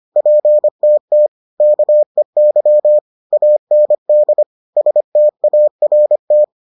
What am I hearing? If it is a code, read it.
Morse code, heard as a keyed tone, about 25 wpm, 600 Hz. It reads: PTT KEY AND START